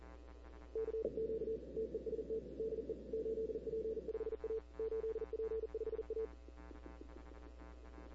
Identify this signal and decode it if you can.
Morse code, heard as a keyed tone, about 40 wpm, 440 Hz. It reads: CQ TEST DE 8P5A 8P5A
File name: CW / 2005-k5tr arrl cw dx -15:00